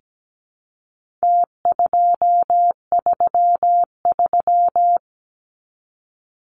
Morse code, heard as a keyed tone, about 17 wpm, 700 Hz. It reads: T233
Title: Numbers - Contest Style - 4-Digit 17wpm